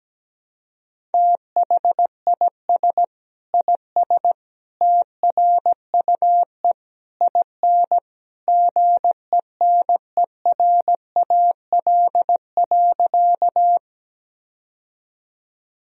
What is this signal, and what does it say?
Morse code, heard as a keyed tone, about 17 wpm, 700 Hz. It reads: THIS IS TRUE IN GENERAL.
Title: Sentences from Top 1000 Words - Review Part 3 - 17wpm